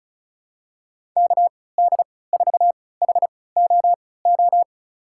Morse code, heard as a keyed tone, about 36 wpm, spaced as for 20 wpm, 700 Hz. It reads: KD4HOO
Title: USA Call Signs - 6 Characters - 3x Character Spacing - 35wpm